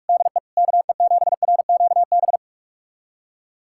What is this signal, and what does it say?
Morse code, heard as a keyed tone, about 46 wpm, 700 Hz. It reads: DE KE7RQB
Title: USA Callsign DE Callsign - 6 Characters - 45wpm